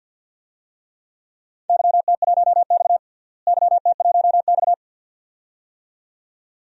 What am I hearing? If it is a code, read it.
Morse code, heard as a keyed tone, about 47 wpm, 700 Hz. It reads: YT1X YT1X
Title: DXCC-100 Call Signs - 4 Characters - Repeated 2x - 50wpm